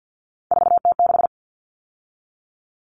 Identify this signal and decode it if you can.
Morse code, heard as a keyed tone, about 50 wpm, 700 Hz. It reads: 4T6
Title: All Cut Numbers - 3-Digit 50wpm